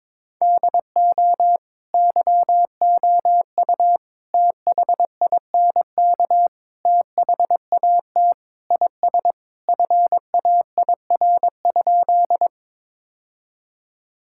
Morse code, heard as a keyed tone, about 22 wpm, 700 Hz. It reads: DO YOU THINK THAT IS FAIR?
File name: practice / 22wpm / Sentences with 4-Word Phrases - 22wpm